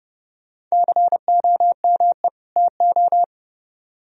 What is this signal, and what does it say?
Morse code, heard as a keyed tone, about 34 wpm, 700 Hz. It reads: COME TO
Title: Speed-Racing - Sets of 2 Words 30wpm